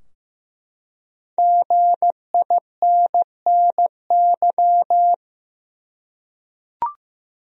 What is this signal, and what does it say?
Morse code, heard as a keyed tone, about 15 wpm, 700 Hz. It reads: GINNY
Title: USA Common Names - 15wpm